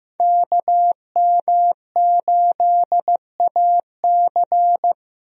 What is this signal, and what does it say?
Morse code, heard as a keyed tone, about 15 wpm, 700 Hz. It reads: KM8AC
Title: USA Call Signs - 5 Characters - Repeated 3x - 15wpm